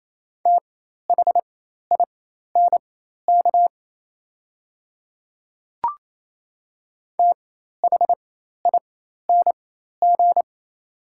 Morse code, heard as a keyed tone, about 28 wpm, spaced as for 13 wpm, 700 Hz. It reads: THINK THING
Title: 4x Character Spacing - Top 200 Words - 28wpm